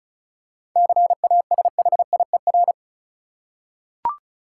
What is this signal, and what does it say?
Morse code, heard as a keyed tone, about 36 wpm, 700 Hz. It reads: CASHIER
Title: Practice QSO - Exchange 3 - More Info - 35wpm